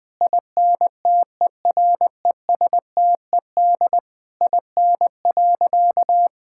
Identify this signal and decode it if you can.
Morse code, heard as a keyed tone, about 20 wpm, 700 Hz. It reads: INTERESTED IN.
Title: Sentences from Top 1000 Words - Review Part 6 - 20wpm